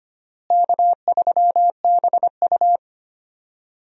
Morse code, heard as a keyed tone, about 25 wpm, 700 Hz. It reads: K3BU
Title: USA Call Signs - 4 Characters - Repeated 3x - 25wpm